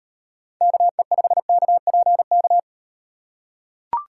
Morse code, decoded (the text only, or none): KE5KPK